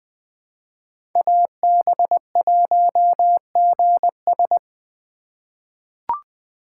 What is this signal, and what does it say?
Morse code, heard as a keyed tone, about 20 wpm, 700 Hz. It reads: AB1GS
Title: USA Call Signs - 5 Characters - 4x Character Spacing - 20wpm